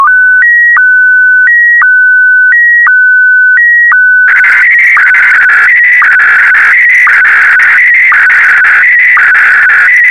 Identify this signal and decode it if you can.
SSTV (slow-scan television), Scottie DX.